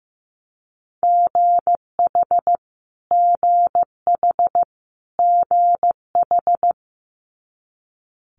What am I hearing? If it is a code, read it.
Morse code, heard as a keyed tone, about 15 wpm, 700 Hz. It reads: GH GH GH
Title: Multi-letter Phonemes Repeated 3x 15wpm